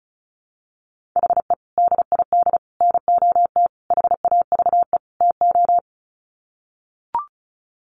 Morse code, heard as a keyed tone, about 34 wpm, 700 Hz. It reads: HE DID NOT HAVE TO
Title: Sets of 5 Words 35wpm